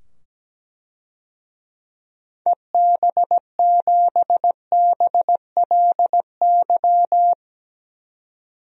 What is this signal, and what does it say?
Morse code, heard as a keyed tone, about 17 wpm, 700 Hz. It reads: EB7BLY
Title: DXCC-100 Call Signs - 6 Characters - 17wpm